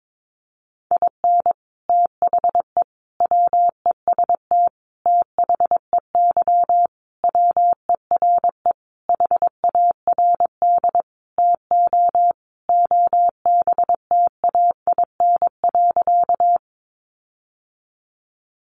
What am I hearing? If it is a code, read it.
Morse code, heard as a keyed tone, about 22 wpm, 700 Hz. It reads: IN THE WEST THEY WERE HARD TO OBTAIN.
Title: Sentences from Top 1300 Words 22wpm